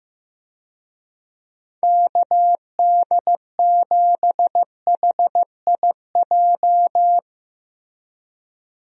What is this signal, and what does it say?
Morse code, heard as a keyed tone, about 15 wpm, 700 Hz. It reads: KD7HIJ
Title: USA Call Signs - 6 Characters - Repeated 2x - 15wpm